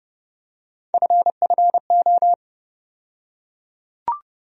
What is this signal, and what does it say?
Morse code, heard as a keyed tone, about 30 wpm, 700 Hz. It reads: FFO
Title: USA Call Sign Postfixes - 3 Letters 30wpm